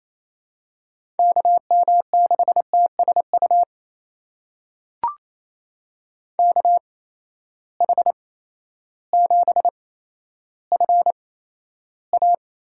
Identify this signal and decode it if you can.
Morse code, heard as a keyed tone, about 28 wpm, spaced as for 13 wpm, 700 Hz. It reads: KM6TSU KH7FA
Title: USA Call Signs - 8x Character Spacing - 28wpm